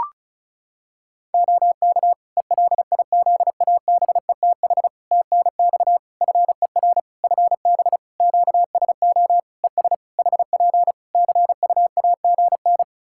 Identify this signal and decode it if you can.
Morse code, heard as a keyed tone, about 36 wpm, 700 Hz. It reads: OK ELIZABETH TNX FER FB QSO ES HP CUAGN
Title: Practice QSO - Exchange 4 - Ending - 35wpm